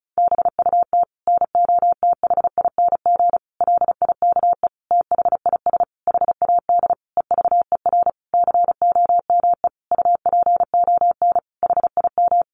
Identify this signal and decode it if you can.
Morse code, heard as a keyed tone, about 34 wpm, 700 Hz. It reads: BUT NOTHING LIKE THIS HAD EVER COME UPON HIM
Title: Sentences from Top 1100 Words 35wpm